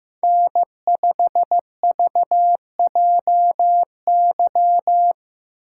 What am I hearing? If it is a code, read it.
Morse code, heard as a keyed tone, about 15 wpm, 700 Hz. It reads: N5VJY
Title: USA Call Signs - 6x Character Spacing - 15wpm